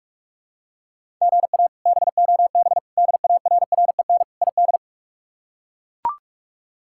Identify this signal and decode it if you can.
Morse code, heard as a keyed tone, about 42 wpm, 700 Hz. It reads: GA BOB DARREN ID
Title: K1USN SST QSO Practice - 45wpm